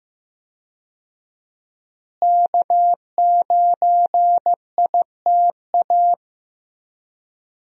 Morse code, heard as a keyed tone, about 15 wpm, 700 Hz. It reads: K9ITA
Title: USA Call Signs - 5 Characters - Repeated 2x - 15wpm